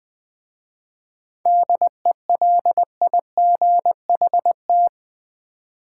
Morse code, heard as a keyed tone, about 20 wpm, 700 Hz. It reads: DELIGHT
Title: 7 Letter Words - First 3 Letters Given - 20wpm